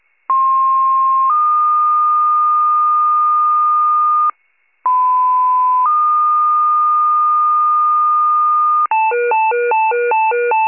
Two-tone page, heard as a one-second tone, then a three-second tone, about 1060 then 1220 Hz, about 1010 then 1220 Hz. It is heard from 0.3 to 8.9 s.